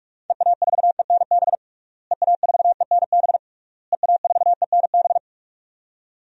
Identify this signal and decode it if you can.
Morse code, heard as a keyed tone, about 46 wpm, 700 Hz. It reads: EA4ENB EA4ENB EA4ENB
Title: DXCC-100 Call Signs - 6 Characters - Repeated 3x - 45wpm